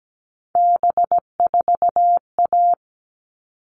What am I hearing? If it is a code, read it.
Morse code, heard as a keyed tone, about 17 wpm, 700 Hz. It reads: B4A
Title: All Cut Numbers - 3-Digit 17wpm